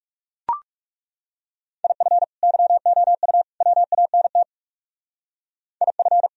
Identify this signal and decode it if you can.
Morse code, heard as a keyed tone, about 37 wpm, 700 Hz. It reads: IF YOU WANT IF
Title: Speed-Racing - Sets of 3 Words 30wpm